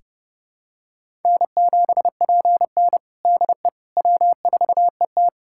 Morse code, heard as a keyed tone, about 30 wpm, 700 Hz. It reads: N7PN DE W4ET
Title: USA Callsign DE Callsign - 4 Characters - 30wpm